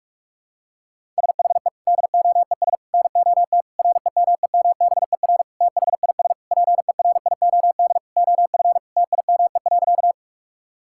Morse code, heard as a keyed tone, about 46 wpm, 700 Hz. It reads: SHE DOES NOT REMEMBER THIS PERIOD OF TIME.
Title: Sentences from Top 1000 Words - Review Part 5 - 45wpm